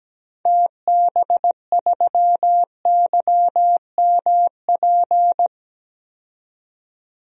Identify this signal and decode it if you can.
Morse code, heard as a keyed tone, about 17 wpm, 700 Hz. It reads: TB3YMP
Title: DXCC-100 Call Signs - 6 Characters - 17wpm